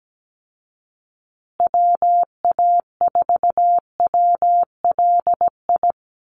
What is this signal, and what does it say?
Morse code, heard as a keyed tone, about 17 wpm, 700 Hz. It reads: WA4WLI
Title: Call Signs Repeated 2x - 17wpm